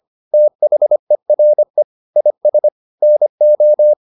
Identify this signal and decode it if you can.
Morse code, heard as a keyed tone, about 25 wpm, 600 Hz. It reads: THERE IS NO